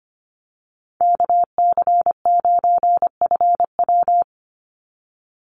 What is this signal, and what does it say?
Morse code, heard as a keyed tone, about 25 wpm, 700 Hz. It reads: KC9FW
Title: USA Call Signs 25wpm